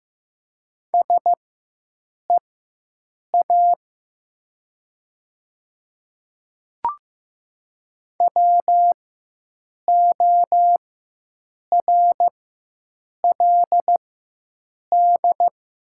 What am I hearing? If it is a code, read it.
Morse code, heard as a keyed tone, about 15 wpm, spaced as for 7 wpm, 700 Hz. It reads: SEA WORLD
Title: 4x Character Spacing - Top 300 Words - 15wpm